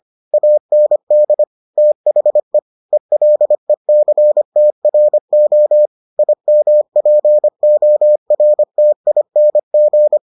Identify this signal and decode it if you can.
Morse code, heard as a keyed tone, about 25 wpm, 600 Hz. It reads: AND THE ELECTRO IMPORTING